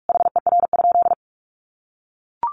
Morse code, heard as a keyed tone, about 44 wpm, 700 Hz. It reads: HER?
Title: Sentences from Top 800 Words 45wpm